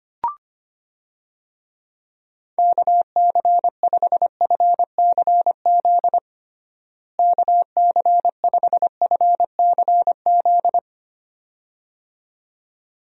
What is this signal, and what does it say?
Morse code, heard as a keyed tone, about 25 wpm, 700 Hz. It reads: KC5FCZ KC5FCZ
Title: USA Call Signs - 6 Characters - Repeated 2x - 25wpm